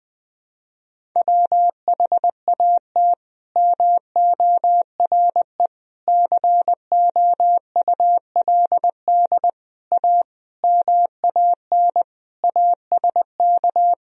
Morse code, heard as a keyed tone, about 21 wpm, 700 Hz. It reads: WHAT MORE COULD A MAN ASK
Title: Sentences from Top 200 Words and Abbreviations 20wpm